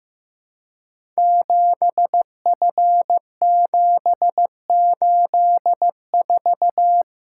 Morse code, heard as a keyed tone, about 15 wpm, 700 Hz. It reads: 7F784
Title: Code Groups - 5 Letter:Numbers - 15wpm